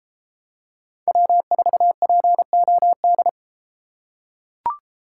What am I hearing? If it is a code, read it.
Morse code, heard as a keyed tone, about 31 wpm, 700 Hz. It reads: W4POD